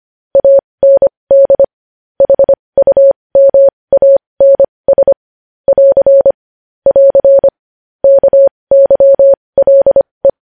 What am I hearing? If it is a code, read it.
Morse code, heard as a keyed tone, about 25 wpm, spaced as for 20 wpm, 550 Hz. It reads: AND HUMANS <AR> <AR> KYLE
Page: News Headlines in Morse Code at 20 WPM